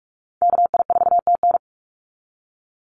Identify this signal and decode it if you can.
Morse code, heard as a keyed tone, about 44 wpm, 700 Hz. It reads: KI4TN
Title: NA SOTA Activator Callsigns 45wpm